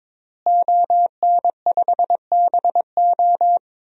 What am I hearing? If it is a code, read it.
Morse code, heard as a keyed tone, about 22 wpm, 700 Hz. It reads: ON5BO
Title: DXCC-100 Call Signs - 5 Characters - Repeated 2x - 22wpm